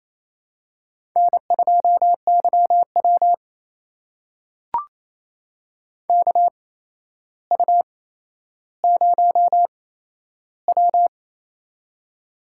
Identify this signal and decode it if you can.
Morse code, heard as a keyed tone, about 28 wpm, spaced as for 12 wpm, 700 Hz. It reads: N2YW KU0W
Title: USA Call Signs - 4 Characters - 8x Character Spacing - 28wpm